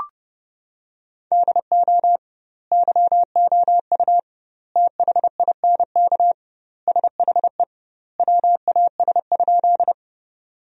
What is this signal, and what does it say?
Morse code, heard as a keyed tone, about 31 wpm, 700 Hz. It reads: DO YOU THINK SHE WAS?
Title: 2x Word-Spacing - Sentences from Top 200 Words - 30wpm